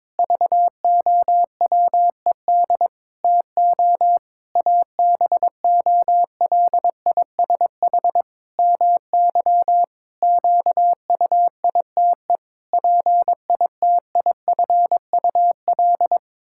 Morse code, heard as a keyed tone, about 22 wpm, 700 Hz. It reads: VOWED TO ABOLISH MY QUITE PITIFUL
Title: Pangram Challenge v2 - 22wpm